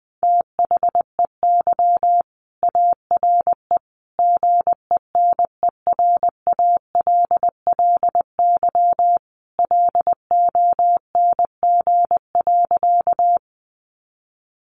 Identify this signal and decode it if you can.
Morse code, heard as a keyed tone, about 20 wpm, 700 Hz. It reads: THEY ARE GENERALLY LONG.
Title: Sentences from Top 600 Words 20wpm